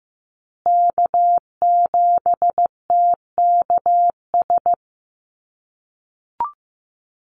Morse code, heard as a keyed tone, about 15 wpm, 700 Hz. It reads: K7TKS
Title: USA Call Signs Repeated 3x 15wpm